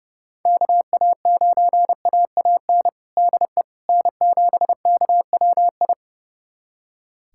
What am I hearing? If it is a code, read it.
Morse code, heard as a keyed tone, about 30 wpm, 700 Hz. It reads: KA9AAN DE N7KWI
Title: USA Callsign DE Callsign 30wpm